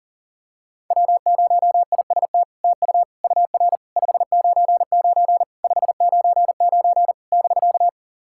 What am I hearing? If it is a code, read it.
Morse code, heard as a keyed tone, about 40 wpm, 700 Hz. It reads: W0IST TU UR 599 599 <BK>
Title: SOTA QSO Practice 40wpm